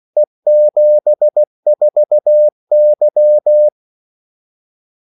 Morse code, heard as a keyed tone, about 16 wpm, 600 Hz. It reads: E74Y
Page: Kaldesignaler 16 wpm